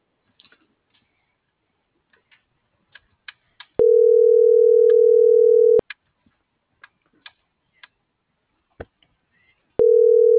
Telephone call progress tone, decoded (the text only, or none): ringback tone